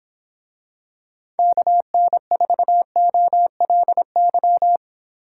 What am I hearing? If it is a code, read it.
Morse code, heard as a keyed tone, about 26 wpm, 700 Hz. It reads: KN4OLY